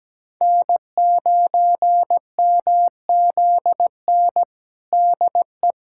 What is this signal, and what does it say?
Morse code, heard as a keyed tone, about 17 wpm, 700 Hz. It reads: N9MZN DE
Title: USA Callsign DE Callsign - 5 Characters - 17wpm